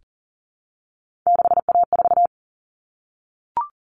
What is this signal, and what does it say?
Morse code, heard as a keyed tone, about 40 wpm, 700 Hz. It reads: 6A4
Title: Common Cut Numbers - 3-Digit 40wpm